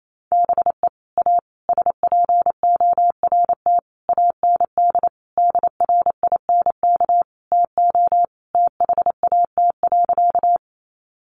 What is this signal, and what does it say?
Morse code, heard as a keyed tone, about 28 wpm, 700 Hz. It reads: BE A SPORT AND DRINK TO THAT.
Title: Sentences from Top 1500 Words 28wpm